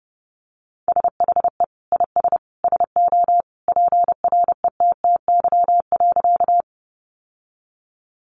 Morse code, heard as a keyed tone, about 30 wpm, 700 Hz. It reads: SHE IS SO PRETTY.
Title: Sentences from Top 1000 Words 30wpm